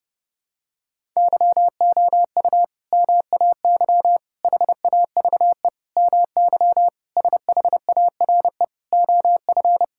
Morse code, heard as a keyed tone, about 30 wpm, 700 Hz. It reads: YOU MAY HAVE MY SHARE OF
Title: Sentences from Top 1000 Words - Review Part 6 - 30wpm